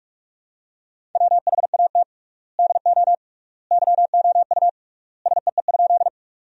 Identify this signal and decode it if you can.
Morse code, heard as a keyed tone, about 46 wpm, 700 Hz. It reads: WHAT DO YOU SEE?
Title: 3x Word-Spacing - Sentences from Top 100 Words - 45wpm